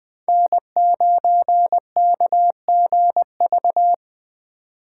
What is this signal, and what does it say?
Morse code, heard as a keyed tone, about 23 wpm, 700 Hz. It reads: N9KGV
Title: Speed Racing - USA 1x3 Call Signs 20wpm